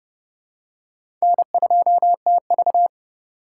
Morse code, heard as a keyed tone, about 30 wpm, 700 Hz. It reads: N2TV
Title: USA Call Signs - 4 Characters - 2x Character Spacing - 30wpm